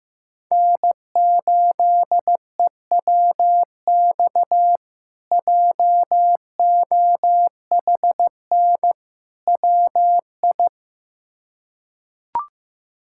Morse code, heard as a keyed tone, about 16 wpm, 700 Hz. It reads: N8EWX JOHN WI
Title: K1USN SST QSO Practice - 15wpm